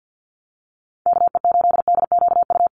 Morse code, heard as a keyed tone, about 50 wpm, 700 Hz. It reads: KE8DQU
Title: USA Call Signs Repeated 3x 50wpm